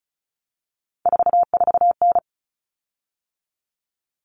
Morse code, heard as a keyed tone, about 34 wpm, 700 Hz. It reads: 44N
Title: Common Cut Numbers - 3-Digit 35wpm